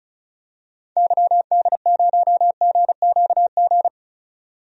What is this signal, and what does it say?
Morse code, heard as a keyed tone, about 34 wpm, 700 Hz. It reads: YD0GQG
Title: DXCC-100 Call Signs - 6 Characters - 35wpm